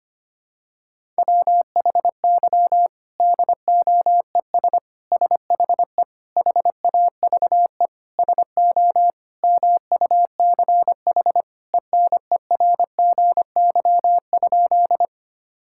Morse code, heard as a keyed tone, about 25 wpm, 700 Hz. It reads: WHY DOES SHE HAVE SO MUCH ENERGY?
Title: Sentences from Top 1000 Words - Review Part 6 - 25wpm